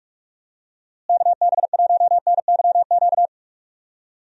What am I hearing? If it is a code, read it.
Morse code, heard as a keyed tone, about 42 wpm, 700 Hz. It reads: KB1NYQ
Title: USA Call Signs - 6 Characters - Repeated 2x - 45wpm